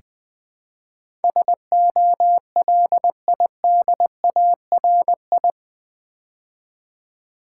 Morse code, heard as a keyed tone, about 20 wpm, 700 Hz. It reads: SOLIDARI
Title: Build A Word - 4 Letters At A Time - 10-Letter Words - 20wpm